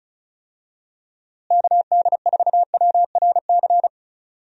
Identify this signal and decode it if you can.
Morse code, heard as a keyed tone, about 36 wpm, 700 Hz. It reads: KD4WRC
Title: USA Call Signs - 6 Characters - Repeated 3x - 35wpm